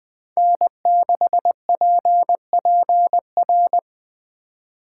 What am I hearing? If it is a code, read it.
Morse code, heard as a keyed tone, about 19 wpm, 700 Hz. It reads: N6PPR